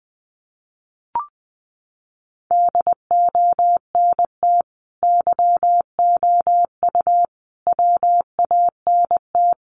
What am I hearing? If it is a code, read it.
Morse code, heard as a keyed tone, about 20 wpm, 700 Hz. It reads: DONT YOU WANT
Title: Sentences from Top 300 Words 20wpm